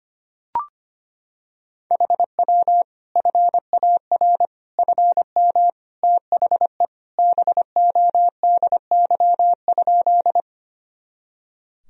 Morse code, heard as a keyed tone, about 25 wpm, 700 Hz. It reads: HW FAR FM THE BODY?
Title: Sentences from Top 400 Words and Abbreviations 25wpm